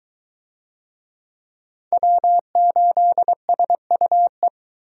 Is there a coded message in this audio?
Yes: Morse code from 1.9 to 4.5 s, heard as a keyed tone, about 22 wpm, 700 Hz. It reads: W8SUE